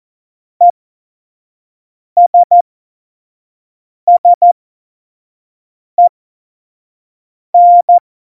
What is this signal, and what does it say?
Morse code, heard as a keyed tone, about 14 wpm, spaced as for 5 wpm, 700 Hz. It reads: ESSEN
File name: Morsekurs / Witze / Witz 03.Dezember 2019, 25 F70